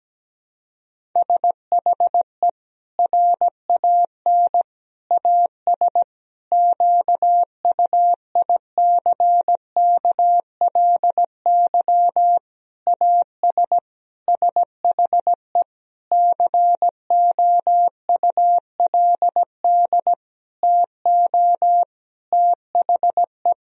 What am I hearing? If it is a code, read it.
Morse code, heard as a keyed tone, about 17 wpm, 700 Hz. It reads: SHE RAN AS QUICKLY AS SHE COULD TO THE
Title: Sentences from Top 1000 Words - Review Part 4 - 17wpm